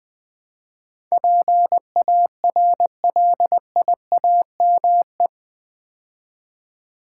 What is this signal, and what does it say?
Morse code, heard as a keyed tone, about 20 wpm, 700 Hz. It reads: PARLIAME
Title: Build A Word - 4 Letters At A Time - 10-Letter Words - 20wpm